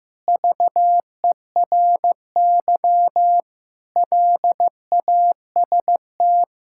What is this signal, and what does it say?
Morse code, heard as a keyed tone, about 15 wpm, 700 Hz. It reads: VERY LAST
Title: Sentences from Top 1600 Words 15wpm